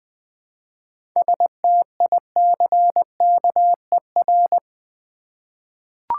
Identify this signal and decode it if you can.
Morse code, heard as a keyed tone, about 20 wpm, 690 Hz. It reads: STICKER
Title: Easy Words with Hints - 20wpm